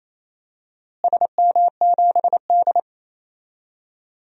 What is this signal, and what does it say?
Morse code, heard as a keyed tone, about 28 wpm, 700 Hz. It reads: SM7D
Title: DXCC-100 Call Signs - 4 Characters - Repeated 2x - 28wpm